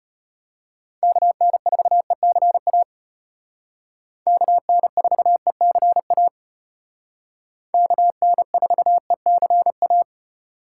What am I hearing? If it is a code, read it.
Morse code, heard as a keyed tone, about 33 wpm, 700 Hz. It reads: KN4ECA KN4ECA KN4ECA